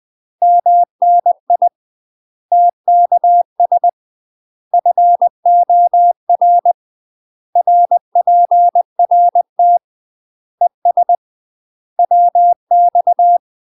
Morse code, heard as a keyed tone, about 20 wpm, 700 Hz. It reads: MNI TKS FOR RPRT ES WX
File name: UK MASTER QSOs TRAINING/20WPM